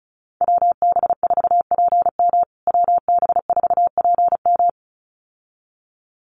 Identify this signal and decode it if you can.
Morse code, heard as a keyed tone, about 34 wpm, 700 Hz. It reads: WB4PM WB4PM
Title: USA Call Signs Repeated 2x 35wpm